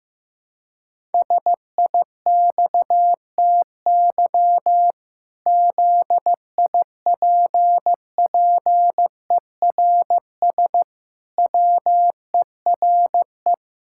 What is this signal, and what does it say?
Morse code, heard as a keyed tone, about 16 wpm, 700 Hz. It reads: SIXTY ZIPPERS WERE
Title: Pangram Challenge v2 - 15wpm